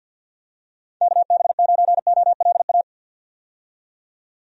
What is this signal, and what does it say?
Morse code, heard as a keyed tone, about 52 wpm, 700 Hz. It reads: KB9OLA
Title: USA Call Signs - 6 Characters - 50wpm